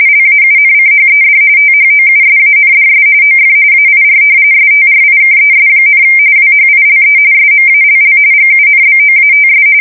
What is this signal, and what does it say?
RTTY, about 45.45 baud. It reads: LAUNCHED THOUSANDS OF AIR STRIKES, WHILE SMALL GUNBOATS
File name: fsk170 / usnavy5